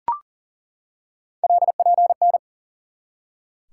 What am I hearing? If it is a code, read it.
Morse code, heard as a keyed tone, about 40 wpm, 690 Hz. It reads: LPN
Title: Practice QSO - Exchange 3 - More Info - 40wpm